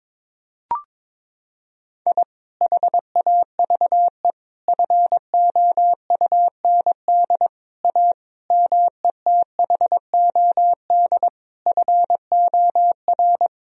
Morse code, heard as a keyed tone, about 22 wpm, 700 Hz. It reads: I HAVE FOUND A METHOD FOR
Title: Sentences from Top 1000 Words - Review Part 3 - 22wpm